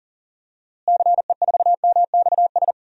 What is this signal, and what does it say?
Morse code, heard as a keyed tone, about 40 wpm, 700 Hz. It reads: CE4MXS
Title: DXCC-100 Call Signs - 6 Characters - Repeated 3x - 40wpm